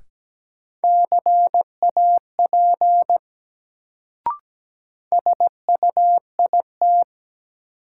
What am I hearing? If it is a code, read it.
Morse code, heard as a keyed tone, about 17 wpm, 700 Hz. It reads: CAP SUIT
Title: Sentences from Top 1600 Words - New Word Drill - 17wpm